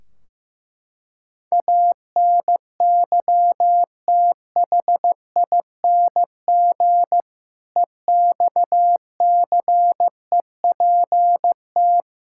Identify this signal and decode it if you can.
Morse code, heard as a keyed tone, about 15 wpm, 700 Hz. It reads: ANYTHING EXCEPT
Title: Sentences from Top 1600 Words 15wpm